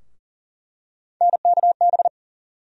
Morse code, heard as a keyed tone, about 40 wpm, 700 Hz. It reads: NKB
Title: USA Call Sign Postfixes - 3 Letters 40wpm